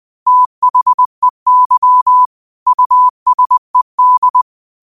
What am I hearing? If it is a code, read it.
Morse code, heard as a keyed tone, about 20 wpm, 1000 Hz. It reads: THEY USED
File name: qotdfiles / 20wpm